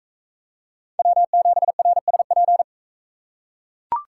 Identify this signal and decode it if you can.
Morse code, heard as a keyed tone, about 44 wpm, 700 Hz. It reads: W7RSP